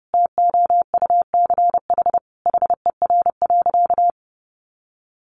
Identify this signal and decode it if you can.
Morse code, heard as a keyed tone, about 28 wpm, 700 Hz. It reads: TOUCH HER.
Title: Sentences from Top 1200 Words 30wpm